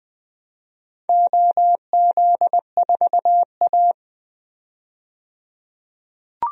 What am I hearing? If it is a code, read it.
Morse code, heard as a keyed tone, about 21 wpm, 700 Hz. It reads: OZ4A
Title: DXCC-100 Call Signs - 4 Characters - 20wpm